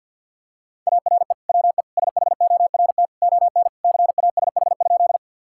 Morse code, heard as a keyed tone, about 51 wpm, 700 Hz. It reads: ARE WE SHORT ON CASH?
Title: Sentences from Top 1000 Words - Review Part 5 - 50wpm